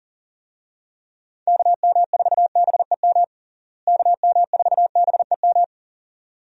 Morse code, heard as a keyed tone, about 40 wpm, 700 Hz. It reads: KM4BEM KM4BEM
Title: USA Call Signs - 6 Characters - Repeated 2x - 40wpm